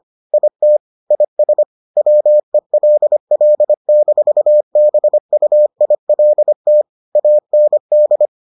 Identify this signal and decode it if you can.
Morse code, heard as a keyed tone, about 25 wpm, 600 Hz. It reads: IT IS WELL-BUILT AND